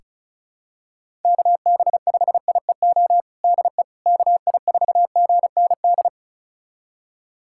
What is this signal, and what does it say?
Morse code, heard as a keyed tone, about 34 wpm, 700 Hz. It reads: KB5IEO DE KI4GND
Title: USA Callsign DE Callsign 35wpm